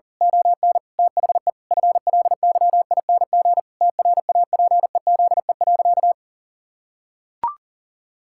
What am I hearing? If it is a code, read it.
Morse code, heard as a keyed tone, about 37 wpm, 700 Hz. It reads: ON THE FLYING TRAPEZE.
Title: Pangram Challenge v2 - 40wpm